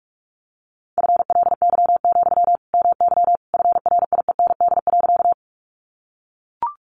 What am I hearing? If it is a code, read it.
Morse code, heard as a keyed tone, about 44 wpm, 700 Hz. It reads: FLY, MY FRIEND.
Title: Sentences from Top 500 Words 45wpm